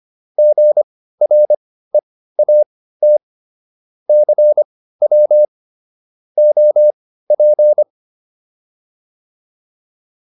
Morse code, heard as a keyed tone, about 25 wpm, spaced as for 15 wpm, 600 Hz. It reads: GREAT CW OP